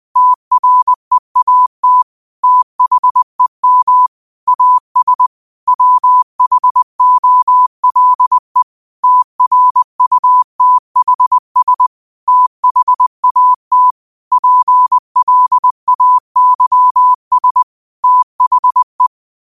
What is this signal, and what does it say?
Morse code, heard as a keyed tone, about 20 wpm, 1000 Hz. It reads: TREAT THEM AS WHOLE TRUTHS THAT PLAYS THE
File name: qotdfiles / 20wpm